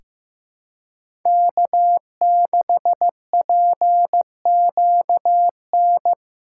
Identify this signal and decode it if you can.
Morse code, heard as a keyed tone, about 15 wpm, 700 Hz. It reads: K6PQN
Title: USA Callsign DE Callsign - 5 Characters - 15wpm